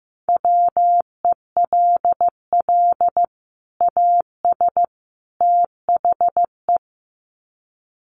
Morse code, heard as a keyed tone, about 14 wpm, 700 Hz. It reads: WELL AS THE
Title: Sets of 3 Words 15wpm